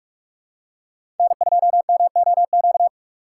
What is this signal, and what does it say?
Morse code, heard as a keyed tone, about 46 wpm, 700 Hz. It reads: N2MOQ
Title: USA Call Signs - 5 Characters - 2x Character Spacing - 45wpm